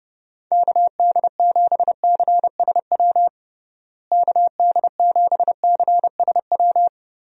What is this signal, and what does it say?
Morse code, heard as a keyed tone, about 30 wpm, 700 Hz. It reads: KD7CSW KD7CSW
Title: USA Call Signs - 6 Characters - Repeated 3x - 30wpm